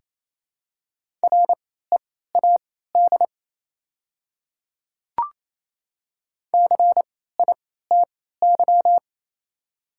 Morse code, heard as a keyed tone, about 28 wpm, spaced as for 16 wpm, 700 Hz. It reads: READ CITY
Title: 3x Character Spacing - Top 300 Words - 28wpm